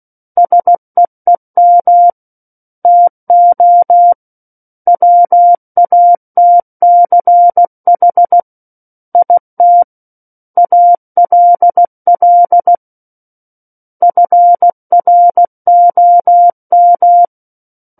Morse code, heard as a keyed tone, about 16 wpm, 700 Hz. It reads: SEEM TO WATCH IT ALL FROM
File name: Book One / 07-Chapter 7 - 16 WPM